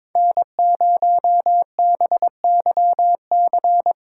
Morse code, heard as a keyed tone, about 24 wpm, 700 Hz. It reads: N0BYC
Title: Speed Racing - USA 1x3 Call Signs 22wpm